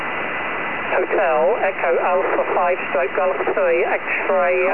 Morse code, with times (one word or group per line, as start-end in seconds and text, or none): none